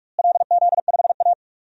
Morse code, heard as a keyed tone, about 45 wpm, 700 Hz. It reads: LZ5A
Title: DXCC-100 Call Signs - 4 Characters - Repeated 3x - 45wpm